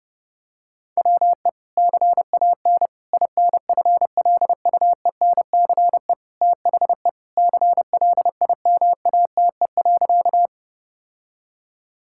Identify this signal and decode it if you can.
Morse code, heard as a keyed tone, about 30 wpm, 700 Hz. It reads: WE CAN INFLUENCE THE CLIMATE.
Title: Sentences from Top 1600 Words 30wpm